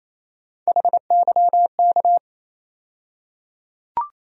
Morse code, heard as a keyed tone, about 28 wpm, 700 Hz. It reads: HYK
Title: USA Call Sign Postfixes - 3 Letters 28wpm